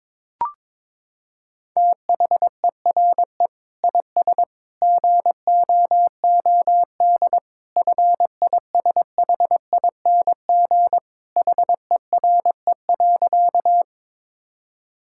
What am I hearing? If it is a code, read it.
Morse code, heard as a keyed tone, about 21 wpm, 700 Hz. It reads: THERE IS GOOD FISHING HERE.
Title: Sentences from Top 1000 Words - Review Part 5 - 22wpm